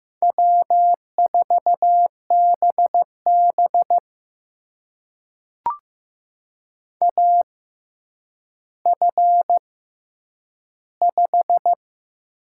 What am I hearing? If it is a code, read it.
Morse code, heard as a keyed tone, about 15 wpm, spaced as for 9 wpm, 700 Hz. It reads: W4BB AF5
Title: USA Call Signs - 6x Character Spacing - 15wpm